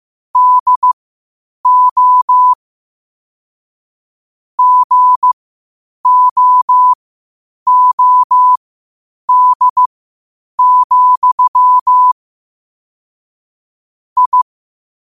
Morse code, heard as a keyed tone, about 15 wpm, spaced as for 8 wpm, 1000 Hz. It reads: DO GOOD, I
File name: qotdfiles / 8wpm